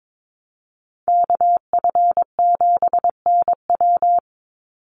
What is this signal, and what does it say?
Morse code, heard as a keyed tone, about 22 wpm, 700 Hz. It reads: KF7NW
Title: USA Call Signs 22wpm